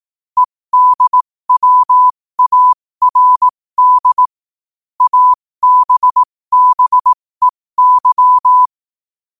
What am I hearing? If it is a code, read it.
Morse code, heard as a keyed tone, about 18 wpm, spaced as for 15 wpm, 1000 Hz. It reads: EDWARD ABBEY